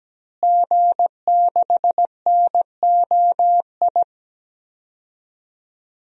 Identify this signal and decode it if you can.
Morse code, heard as a keyed tone, about 17 wpm, 700 Hz. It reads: G6NOI
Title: DXCC-100 Call Signs - 5 Characters - Repeated 3x - 17wpm